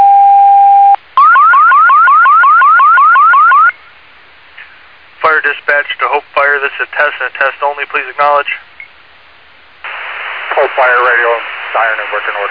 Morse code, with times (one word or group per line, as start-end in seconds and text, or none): none